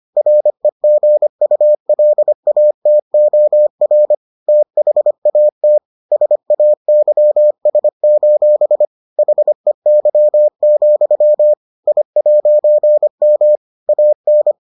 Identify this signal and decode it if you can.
Morse code, heard as a keyed tone, about 25 wpm, 600 Hz. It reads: REGULATOR THAT SAYS: HEY, I'M AN